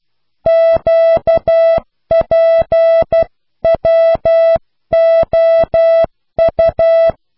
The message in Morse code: QPWOU